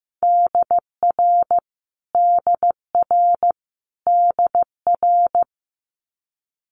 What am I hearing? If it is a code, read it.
Morse code, heard as a keyed tone, about 15 wpm, 700 Hz. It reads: DR DR DR
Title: Multi-letter Phonemes Repeated 3x 15wpm